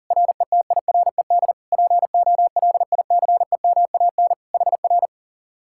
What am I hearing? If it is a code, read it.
Morse code, heard as a keyed tone, about 50 wpm, 700 Hz. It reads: RETIRED POLICEMAN HR
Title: Speed-Racing Common QSO Phrases - 40wpm